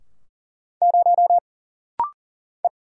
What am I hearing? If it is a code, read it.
Morse code, heard as a keyed tone, about 40 wpm, 700 Hz. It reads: 0 E
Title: Single Letter-Number Alt - Rapid-Fire 40wpm